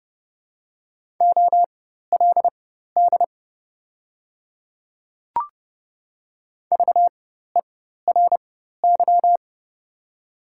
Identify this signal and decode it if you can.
Morse code, heard as a keyed tone, about 30 wpm, spaced as for 14 wpm, 700 Hz. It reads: OLD VERY
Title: 4x Character Spacing - Top 300 Words - 30wpm